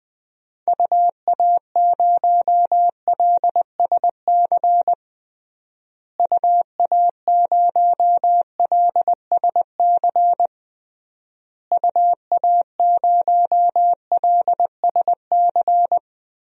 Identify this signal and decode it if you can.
Morse code, heard as a keyed tone, about 20 wpm, 700 Hz. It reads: UA0LSC UA0LSC UA0LSC
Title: DXCC-100 Call Signs - 6 Characters - Repeated 3x - 20wpm